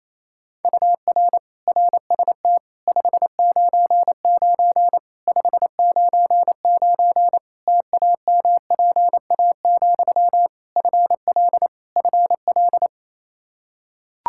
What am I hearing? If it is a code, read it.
Morse code, heard as a keyed tone, about 28 wpm, 700 Hz. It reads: UR RST 599 599 TAMPA, FL FL
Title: K1USN SST QSO Practice 28wpm